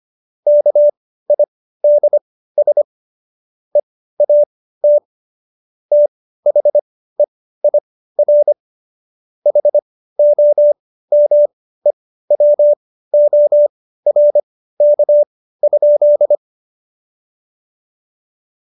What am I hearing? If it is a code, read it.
Morse code, heard as a keyed tone, about 25 wpm, spaced as for 15 wpm, 590 Hz. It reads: KIDS EAT THEIR HOMEWORK?
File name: Jokes-25-char-15-eff